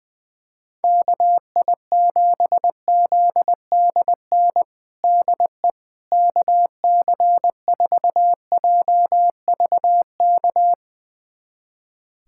Morse code, heard as a keyed tone, about 20 wpm, 700 Hz. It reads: KI7ZDN DE KC4JVK
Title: USA Callsign DE Callsign - 6 Characters - 20wpm